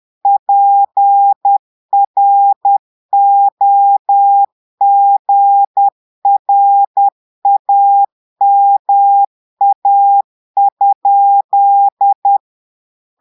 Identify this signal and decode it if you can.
Morse code, heard as a keyed tone, about 10 wpm, 800 Hz. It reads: PROGRAMA?
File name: portugais texte en clair 2 vitesse 10 mots minutes